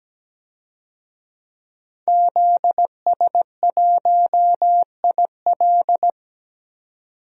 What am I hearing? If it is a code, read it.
Morse code, heard as a keyed tone, about 17 wpm, 700 Hz. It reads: ZS1IL
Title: DXCC-100 Call Signs - 5 Characters - 17wpm